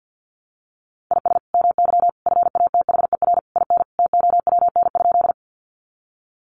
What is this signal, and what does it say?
Morse code, heard as a keyed tone, about 50 wpm, 700 Hz. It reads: IS MY FATHER IN TOWN?
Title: Sentences from Top 500 Words 50wpm